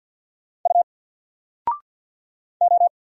Morse code, heard as a keyed tone, about 46 wpm, 700 Hz. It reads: U O
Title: Single Letter-Number 50wpm